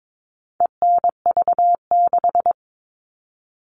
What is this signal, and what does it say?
Morse code, heard as a keyed tone, about 22 wpm, 700 Hz. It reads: ENV6
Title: All Cut Numbers - 4-Digit 22wpm